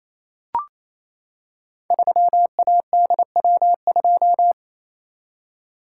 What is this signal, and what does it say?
Morse code, heard as a keyed tone, about 27 wpm, 700 Hz. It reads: 3ADW2
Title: Code Groups - 5 Letter:Numbers - 28wpm